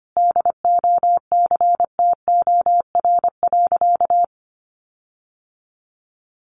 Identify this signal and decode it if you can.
Morse code, heard as a keyed tone, about 25 wpm, 700 Hz. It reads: DOCTOR.
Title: Sentences from Top 1500 Words 25wpm